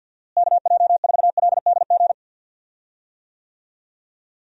Morse code, heard as a keyed tone, about 52 wpm, 700 Hz. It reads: KJ4LDG
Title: USA Call Signs - 6 Characters - Repeated 3x - 50wpm